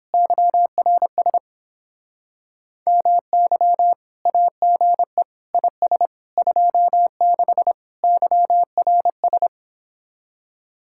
Morse code, heard as a keyed tone, about 29 wpm, 700 Hz. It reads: YRS MY AGE IS 26 YRS